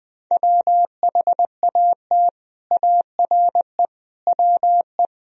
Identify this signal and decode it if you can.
Morse code, heard as a keyed tone, about 21 wpm, 700 Hz. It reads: WHAT ARE WE